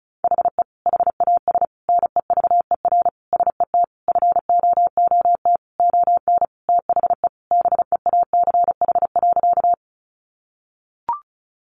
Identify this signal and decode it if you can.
Morse code, heard as a keyed tone, about 34 wpm, 700 Hz. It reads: HE HAS NEVER SET FOOT ON THE BEACH.
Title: Sentences from Top 1500 Words 35wpm